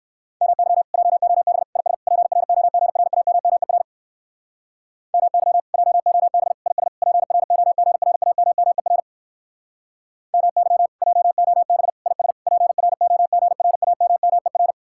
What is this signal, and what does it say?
Morse code, heard as a keyed tone, about 59 wpm, 700 Hz. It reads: MY JOB IS PROGRAMMER MY JOB IS PROGRAMMER MY JOB IS PROGRAMMER